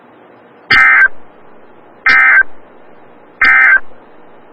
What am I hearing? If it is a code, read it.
Emergency Alert System (SAME) tones, heard as harsh buzzing bursts. It is heard from 0.6 to 3.9 s.
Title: Des Moines, IA NOAA Weather Radio - WXL57